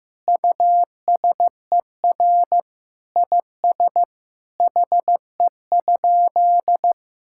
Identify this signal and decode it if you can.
Morse code, heard as a keyed tone, about 15 wpm, 700 Hz. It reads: USER IS HE?
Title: Sentences from Top 1000 Words - Review Part 5 - 15wpm